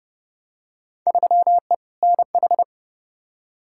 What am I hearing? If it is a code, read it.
Morse code, heard as a keyed tone, about 30 wpm, 700 Hz. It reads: 3E NH
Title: Field Day Exchanges Encore - 30wpm